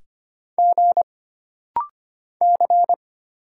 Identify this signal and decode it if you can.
Morse code, heard as a keyed tone, about 25 wpm, 700 Hz. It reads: G C
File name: Single Character (Letter:Number:Punctuation:Prosign) - Rapid-Fire - 25wpm